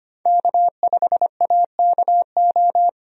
Morse code, heard as a keyed tone, about 25 wpm, 700 Hz. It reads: K5AKO
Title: USA Call Signs - 5 Characters - 25wpm